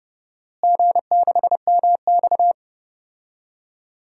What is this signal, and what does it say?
Morse code, heard as a keyed tone, about 30 wpm, 700 Hz. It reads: G6MX
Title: DXCC-100 Call Signs - 4 Characters - Repeated 3x - 30wpm